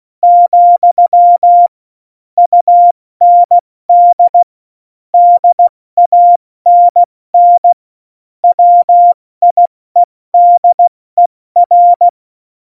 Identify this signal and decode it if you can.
Morse code, heard as a keyed tone, about 16 wpm, 700 Hz. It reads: , UND DANN WIEDER
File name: Morsekurs / Witze / MP3, Witz 08. Mai 2018, 70 F80